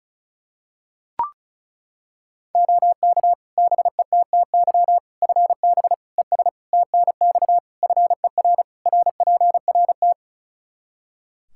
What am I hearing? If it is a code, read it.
Morse code, heard as a keyed tone, about 36 wpm, 700 Hz. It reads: OK BETTY FB ES TNX FER RPRT
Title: Practice QSO - Exchange 2 - Info - 35wpm